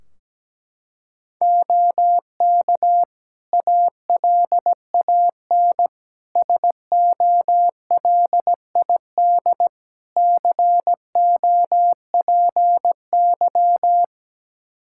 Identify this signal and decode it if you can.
Morse code, heard as a keyed tone, about 17 wpm, 700 Hz. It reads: OK ALAN SOLID COPY
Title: Practice QSO - Exchange 3 - More Info - 17wpm